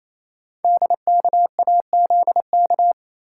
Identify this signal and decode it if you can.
Morse code, heard as a keyed tone, about 27 wpm, 700 Hz. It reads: DKAZK
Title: Code Groups - 5 Letters - 28wpm